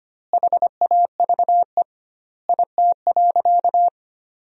Morse code, heard as a keyed tone, about 25 wpm, 700 Hz. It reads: HAVE IT.
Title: 2x Word-Spacing - Sentences from Top 100 Words - 25wpm